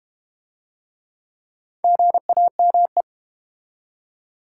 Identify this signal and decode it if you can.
Morse code, heard as a keyed tone, about 32 wpm, 700 Hz. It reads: GAME